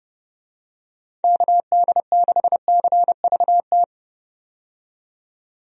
Morse code, heard as a keyed tone, about 30 wpm, 700 Hz. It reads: KD6CVT
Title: USA Call Signs - 6 Characters - 30wpm